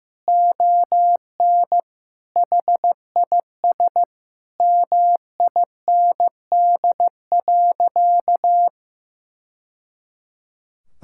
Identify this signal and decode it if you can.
Morse code, heard as a keyed tone, about 15 wpm, 700 Hz. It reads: ON HIS MIND.
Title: Sentences from Top 1000 Words - Review Part 5 - 15wpm